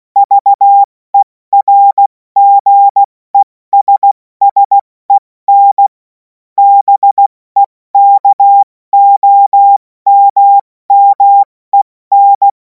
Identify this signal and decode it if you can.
Morse code, heard as a keyed tone, about 16 wpm, 800 Hz. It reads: VERGESSEN BEKOMMEN
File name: Morsekurs / Witze / Witz 03. Februar 2026 70 F80